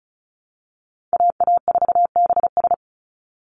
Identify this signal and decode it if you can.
Morse code, heard as a keyed tone, about 34 wpm, 700 Hz. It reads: AA4BS
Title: USA Call Signs 35wpm